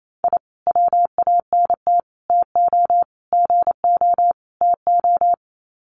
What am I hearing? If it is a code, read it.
Morse code, heard as a keyed tone, about 28 wpm, 700 Hz. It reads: I WANT TO GO TO
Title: Sets of 5 Words Encore 28wpm